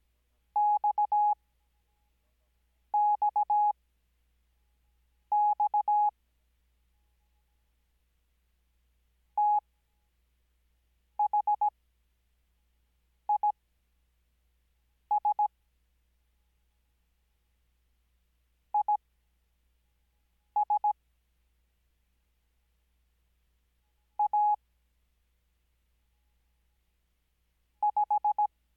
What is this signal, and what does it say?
Morse code, heard as a keyed tone, about 17 wpm, spaced as for 5 wpm, 840 Hz. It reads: XXX THIS IS A 5
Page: Farnsworth Spacing: 15WPM Character speed Spaced at 5WPM.